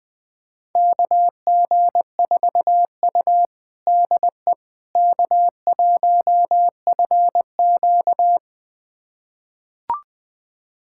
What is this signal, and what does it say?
Morse code, heard as a keyed tone, about 20 wpm, 700 Hz. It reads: KG4U DE K1FQ
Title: USA Callsign DE Callsign - 4 Characters - 20wpm